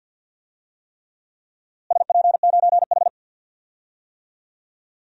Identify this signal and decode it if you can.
Morse code, heard as a keyed tone, about 48 wpm, 700 Hz. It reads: SP9H
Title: DXCC-100 Call Signs - 4 Characters - Repeated 3x - 50wpm